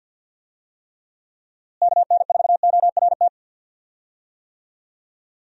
Morse code, heard as a keyed tone, about 51 wpm, 700 Hz. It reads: KN4ORT
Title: USA Call Signs - 6 Characters - Repeated 2x - 50wpm